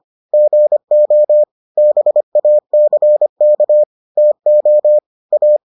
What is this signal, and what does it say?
Morse code, heard as a keyed tone, about 25 wpm, 600 Hz. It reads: GO BACK TO A